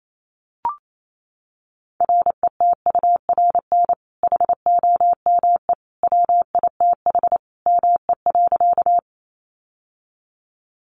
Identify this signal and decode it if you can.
Morse code, heard as a keyed tone, about 28 wpm, 700 Hz. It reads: RETURN HOME WITH ME.
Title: Sentences from Top 800 Words 28wpm